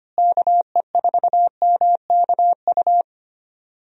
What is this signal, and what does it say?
Morse code, heard as a keyed tone, about 25 wpm, 700 Hz. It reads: KE4MKU
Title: USA Call Signs - 6 Characters - Repeated 3x - 25wpm